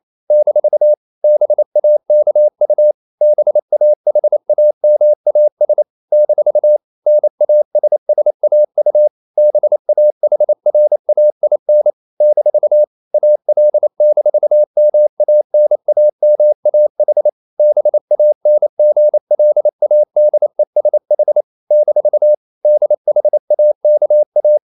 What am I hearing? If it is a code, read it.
Morse code, heard as a keyed tone, about 28 wpm, 600 Hz. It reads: - BAKU BAHAMAS - NASSAU BAHRAIN - AL-MANAMAH BANGLADESH - DHAKA
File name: CW / Practice / NationalCapitals28